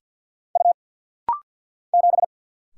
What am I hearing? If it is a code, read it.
Morse code, heard as a keyed tone, about 50 wpm, 700 Hz. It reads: U 7
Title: Single Letter-Number Alt - Rapid-Fire 50wpm